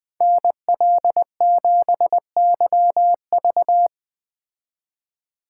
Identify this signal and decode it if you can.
Morse code, heard as a keyed tone, about 21 wpm, 700 Hz. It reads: NL7YV